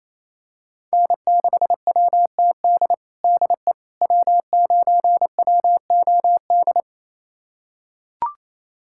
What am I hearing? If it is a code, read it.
Morse code, heard as a keyed tone, about 28 wpm, 700 Hz. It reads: N6WTD DE W9WOD
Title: USA Callsign DE Callsign - 5 Characters - 28wpm